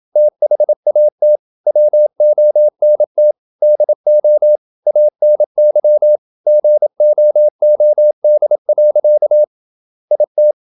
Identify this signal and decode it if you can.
Morse code, heard as a keyed tone, about 27 wpm, 600 Hz. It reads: THAT WONT DO ANY GOOD. IT
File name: Never 0027wpm